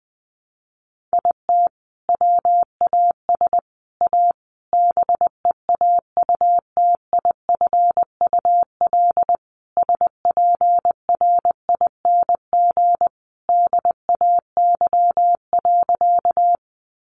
Morse code, heard as a keyed tone, about 20 wpm, 700 Hz. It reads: IT WAS A BEAUTIFUL SPRING DAY.
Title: Sentences from Top 1200 Words 20wpm